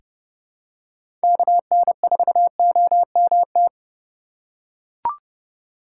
Morse code, heard as a keyed tone, about 30 wpm, 700 Hz. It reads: KN4OMT
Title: USA Call Signs - 6 Characters - 4x Character Spacing - 30wpm